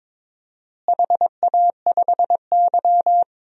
Morse code, heard as a keyed tone, about 23 wpm, 700 Hz. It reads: HA5Y
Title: DXCC-100 Call Signs - 4 Characters - Repeated 3x - 22wpm